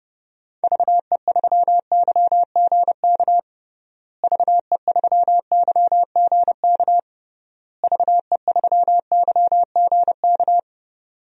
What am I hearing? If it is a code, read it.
Morse code, heard as a keyed tone, about 32 wpm, 700 Hz. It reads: VE3YGK VE3YGK VE3YGK
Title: DXCC-100 Call Signs - 6 Characters - Repeated 3x - 30wpm